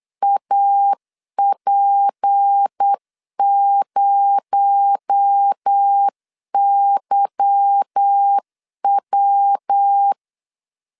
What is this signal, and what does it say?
Morse code, heard as a keyed tone, about 8 wpm, 800 Hz. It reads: AP0YW
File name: CW / DET / 1 9 1500C S A 8wpm 5637